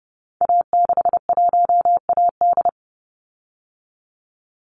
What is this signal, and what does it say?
Morse code, heard as a keyed tone, about 30 wpm, 700 Hz. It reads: A61AD
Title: Call Signs Repeated 3x - 30wpm